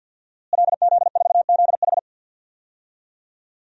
Morse code, heard as a keyed tone, about 52 wpm, 700 Hz. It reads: LZ4ZH
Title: DXCC-100 Call Signs - 5 Characters - Repeated 3x - 50wpm